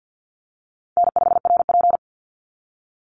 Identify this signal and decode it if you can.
Morse code, heard as a keyed tone, about 51 wpm, 700 Hz. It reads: N5RP
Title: Call Signs - 50wpm